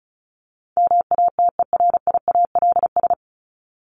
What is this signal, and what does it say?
Morse code, heard as a keyed tone, about 34 wpm, 700 Hz. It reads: MATERIALS
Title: ES:S Suffix - Top 100 Words with Base Words 35wpm